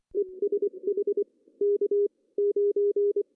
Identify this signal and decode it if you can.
Morse code, heard as a keyed tone, about 27 wpm, spaced as for 23 wpm, 380 Hz. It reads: ESH K 9
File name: CW / cqww-cw-k5tr